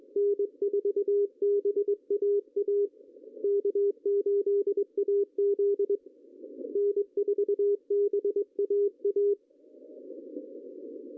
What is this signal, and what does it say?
Morse code, heard as a keyed tone, about 22 wpm, 410 Hz. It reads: N4BAA K8AZ N4BAA
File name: CW / cqww-cw-k5tr